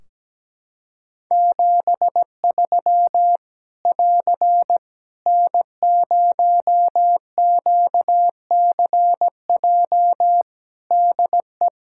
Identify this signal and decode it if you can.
Morse code, heard as a keyed tone, about 17 wpm, 700 Hz. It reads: 73 <AR> N0QCJ DE
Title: Practice QSO - Exchange 4 - Ending - 17wpm